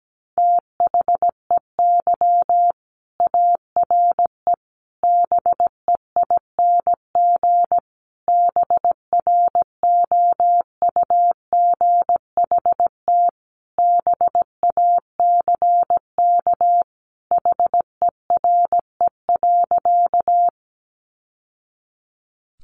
Morse code, heard as a keyed tone, about 17 wpm, 700 Hz. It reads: THEY ARE BEING BROUGHT BACK HERE.
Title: Sentences from Top 600 Words 17wpm